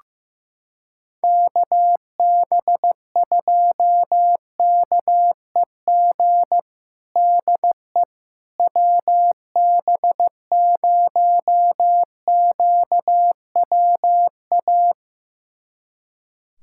Morse code, heard as a keyed tone, about 15 wpm, 700 Hz. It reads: KB2KEG DE WB0QWA
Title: USA Callsign DE Callsign 15wpm